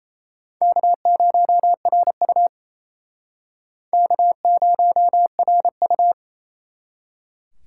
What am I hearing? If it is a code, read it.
Morse code, heard as a keyed tone, about 36 wpm, 700 Hz. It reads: K0RU K0RU